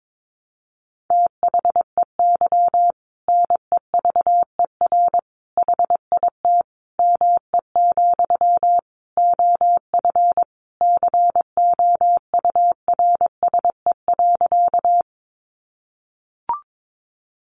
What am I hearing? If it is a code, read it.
Morse code, heard as a keyed tone, about 22 wpm, 700 Hz. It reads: THEY NEVER HIT ME, OF COURSE.
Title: Sentences from Top 1200 Words 22wpm